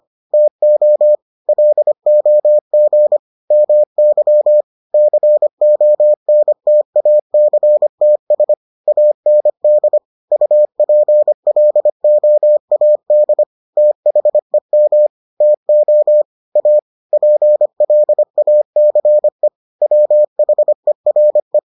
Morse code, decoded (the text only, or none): TO LOG MY CONTACTS AND UPLOAD THEM TO A PLACE WHERE